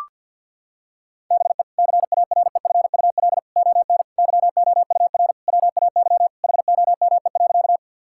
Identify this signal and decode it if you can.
Morse code, heard as a keyed tone, about 50 wpm, 700 Hz. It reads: BE CAREFUL ON YOUR WAY HOME.
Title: Sentences from Top 1600 Words 50wpm